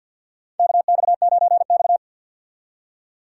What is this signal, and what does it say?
Morse code, heard as a keyed tone, about 48 wpm, 700 Hz. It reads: KX9X
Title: USA Call Signs - 4 Characters - 2x Character Spacing - 50wpm